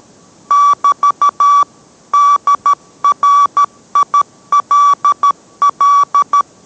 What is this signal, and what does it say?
Morse code, heard as a keyed tone, about 13 wpm, 1150 Hz. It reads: <BT> DRILL